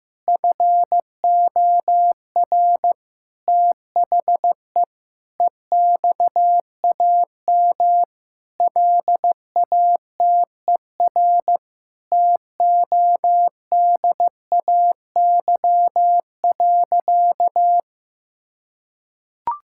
Morse code, heard as a keyed tone, about 16 wpm, 700 Hz. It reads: FOR THE EXAM LATER TODAY.
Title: Sentences from Top 1000 Words - Review Part 5 - 15wpm